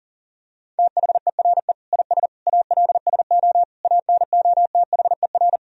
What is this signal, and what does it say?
Morse code, heard as a keyed tone, about 37 wpm, 700 Hz. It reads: THERE IS ALSO ANOTHER
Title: Sentences from Top 1000 Words - Review Part 1 - 40wpm